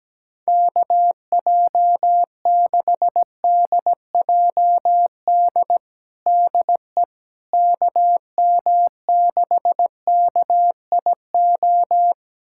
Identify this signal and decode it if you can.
Morse code, heard as a keyed tone, about 17 wpm, 700 Hz. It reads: KJ6DJD DE KM6KIO
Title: USA Callsign DE Callsign 17wpm